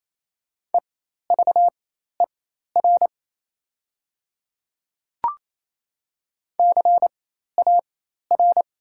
Morse code, heard as a keyed tone, about 28 wpm, spaced as for 13 wpm, 700 Hz. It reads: EVER CAR
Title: 4x Character Spacing - Top 500 Words - 28wpm